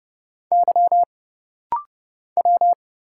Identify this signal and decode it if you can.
Morse code, heard as a keyed tone, about 30 wpm, 700 Hz. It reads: Y W
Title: Single Letters - Rapid-Fire 30wpm